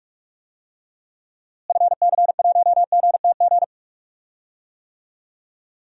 Morse code, heard as a keyed tone, about 46 wpm, 700 Hz. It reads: FC1GTG
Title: DXCC-100 Call Signs - 6 Characters - Repeated 3x - 45wpm